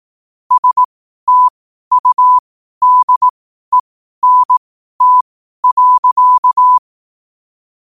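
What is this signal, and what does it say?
Morse code, heard as a keyed tone, about 18 wpm, spaced as for 12 wpm, 1000 Hz. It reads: STUDENT.
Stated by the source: Morse code at 12 WPM.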